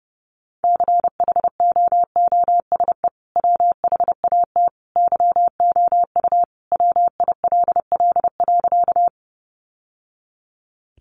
Morse code, heard as a keyed tone, about 30 wpm, 700 Hz. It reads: CHOOSE WHAT YOU WILL.
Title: Sentences from Top 700 Words 30wpm